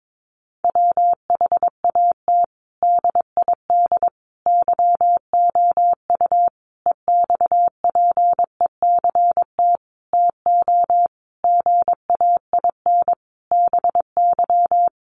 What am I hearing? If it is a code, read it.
Morse code, heard as a keyed tone, about 22 wpm, 700 Hz. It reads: WHAT DID YOU EXPECT TO GAIN BY
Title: Sentences from Top 1100 Words 22wpm